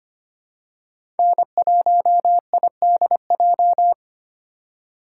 Morse code, heard as a keyed tone, about 25 wpm, 700 Hz. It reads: N1IDJ
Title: USA Call Signs - 5 Characters - 2x Character Spacing - 25wpm